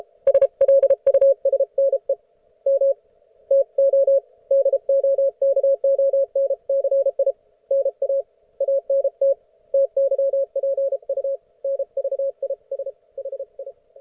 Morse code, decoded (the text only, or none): SLUSNE M TO DOKONCI NA ANT TYPU NVIS HI